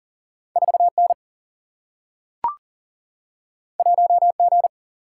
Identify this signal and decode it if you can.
Morse code, heard as a keyed tone, about 40 wpm, 700 Hz. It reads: 4N 1G
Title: Single-Letter Single-Number 40wpm